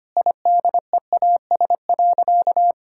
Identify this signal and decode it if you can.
Morse code, heard as a keyed tone, about 25 wpm, 700 Hz. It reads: IDEAS.
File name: practice / 25wpm / Sentences with 2-Word Phrases - 25wpm